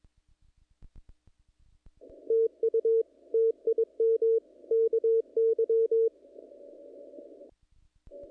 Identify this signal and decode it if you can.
Morse code, heard as a keyed tone, about 24 wpm, 450 Hz. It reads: TU TIM KY
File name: contests / naqp-cw-k5tr-op-k5pi- -1900